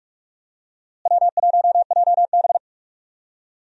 Morse code, heard as a keyed tone, about 46 wpm, 700 Hz. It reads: W1JB
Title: USA Call Signs - 4 Characters - Repeated 2x - 45wpm